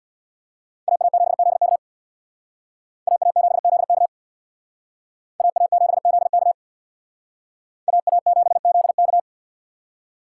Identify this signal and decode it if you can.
Morse code, heard as a keyed tone, about 62 wpm, 700 Hz. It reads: AA7ZK AA7ZK AA7ZK AA7ZK
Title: Speed Racing - USA Call Signs 50wpm